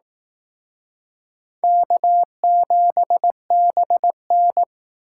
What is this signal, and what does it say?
Morse code, heard as a keyed tone, about 18 wpm, 700 Hz. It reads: K7BN
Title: Speed Racing - USA Call Signs 15wpm